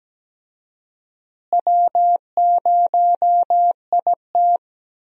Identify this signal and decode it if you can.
Morse code, heard as a keyed tone, about 17 wpm, 700 Hz. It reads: W0IT
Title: USA Call Signs - 4 Characters - Repeated 2x - 17wpm